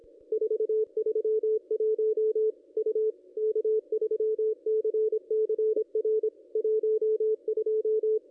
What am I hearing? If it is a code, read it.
Morse code, heard as a keyed tone, about 25 wpm, 440 Hz, with radio noise. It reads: 431 U K3CCR 12
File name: contests / ss-cw-k5tr-op-n5rz -1400